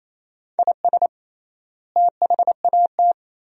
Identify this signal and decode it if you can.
Morse code, heard as a keyed tone, about 28 wpm, 700 Hz. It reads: IS THAT
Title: 3x Word-Spacing - Sentences from Top 500 Words - 28wpm